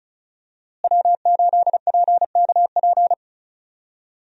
Morse code, heard as a keyed tone, about 36 wpm, 700 Hz. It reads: W8PKP
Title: USA Call Signs - 5 Characters - Repeated 2x - 35wpm